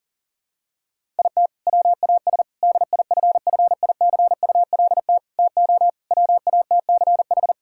Morse code, heard as a keyed tone, about 40 wpm, 700 Hz. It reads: IT WAS DIFFICULT TO WATCH
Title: Sentences from Top 1000 Words - Review Part 5 - 40wpm